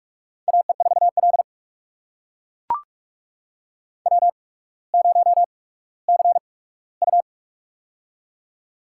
Morse code, heard as a keyed tone, about 41 wpm, spaced as for 19 wpm, 700 Hz. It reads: AE4L W0CU
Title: USA Call Signs - 4 Characters - 8x Character Spacing - 45wpm